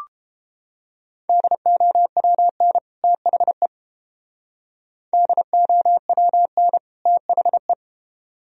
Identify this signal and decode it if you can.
Morse code, heard as a keyed tone, about 31 wpm, 700 Hz. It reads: DOWN THE DOWN THE